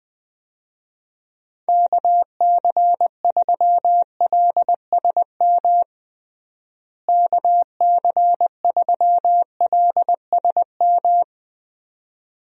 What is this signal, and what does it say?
Morse code, heard as a keyed tone, about 20 wpm, 700 Hz. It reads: KC3LSM KC3LSM
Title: USA Call Signs - 6 Characters - Repeated 2x - 20wpm